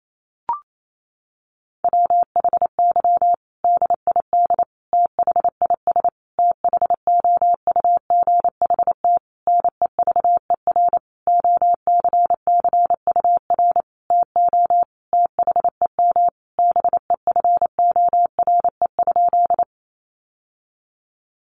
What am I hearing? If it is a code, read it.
Morse code, heard as a keyed tone, about 28 wpm, 700 Hz. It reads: WHY DID THIS THOUGHT NEVER OCCUR TO THEM BEFORE?
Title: Sentences from Top 900 Words 28wpm